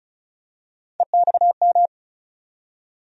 Morse code, heard as a keyed tone, about 36 wpm, 700 Hz. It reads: EXM
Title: USA Call Sign Postfixes - 3 Letters 35wpm